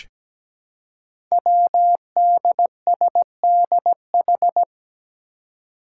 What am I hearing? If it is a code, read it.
Morse code, heard as a keyed tone, about 17 wpm, 700 Hz. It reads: WDSDH
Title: Five Challenging Letter-Number Combinations 17wpm